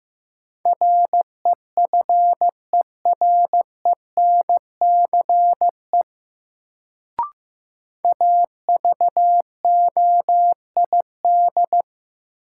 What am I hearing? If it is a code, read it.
Morse code, heard as a keyed tone, about 15 wpm, 700 Hz. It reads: REFERENCE AVOID
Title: Sentences from Top 1200 Words - New Word Drill Alt - 15wpm